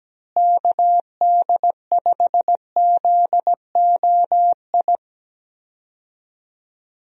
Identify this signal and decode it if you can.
Morse code, heard as a keyed tone, about 17 wpm, 700 Hz. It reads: KD5ZOI
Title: USA Call Signs - 6 Characters - Repeated 3x - 17wpm